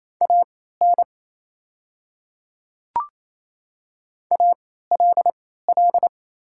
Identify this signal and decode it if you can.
Morse code, heard as a keyed tone, about 27 wpm, spaced as for 16 wpm, 700 Hz. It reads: AN ALL
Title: 3x Character Spacing - Top 100 Words - 28wpm